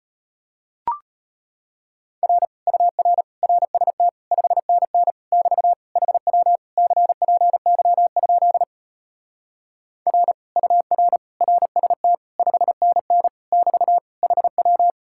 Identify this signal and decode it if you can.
Morse code, heard as a keyed tone, about 33 wpm, 700 Hz. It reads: R UR RST 5NN <BT> HW CPY? R UR RST 5NN <BT> HW